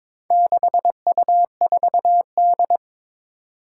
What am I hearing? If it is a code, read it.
Morse code, heard as a keyed tone, about 22 wpm, 700 Hz. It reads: 6U4D
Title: Four Challenging Letter-Number Combinations 22wpm